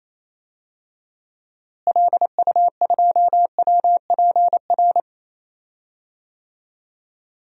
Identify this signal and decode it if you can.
Morse code, heard as a keyed tone, about 28 wpm, 700 Hz. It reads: LU2WPR
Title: DXCC-100 Call Signs - 6 Characters - Repeated 2x - 28wpm